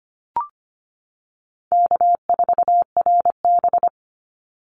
Morse code, heard as a keyed tone, about 25 wpm, 700 Hz. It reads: K4RB
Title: Easy Call Signs - 25wpm